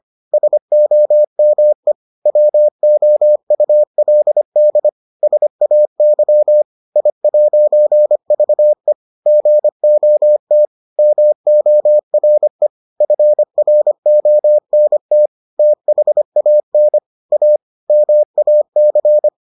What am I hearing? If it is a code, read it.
Morse code, heard as a keyed tone, about 25 wpm, 600 Hz. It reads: SOME WOULD SAY I'VE GOT MORE FRONT THAN A MAC